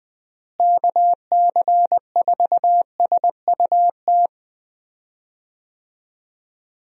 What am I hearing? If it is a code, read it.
Morse code, heard as a keyed tone, about 20 wpm, 700 Hz. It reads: KC4SUT
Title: USA Call Signs - 6 Characters - Repeated 2x - 20wpm